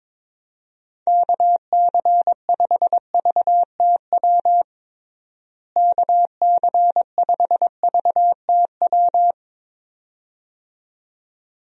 Morse code, heard as a keyed tone, about 22 wpm, 700 Hz. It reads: KC5VTW KC5VTW
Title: USA Call Signs - 6 Characters - Repeated 2x - 22wpm